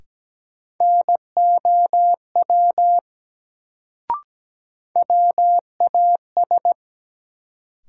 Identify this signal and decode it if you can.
Morse code, heard as a keyed tone, about 17 wpm, 700 Hz. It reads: NOW WAS
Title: Sentences from Top 100 Words - New Word Drill - 17wpm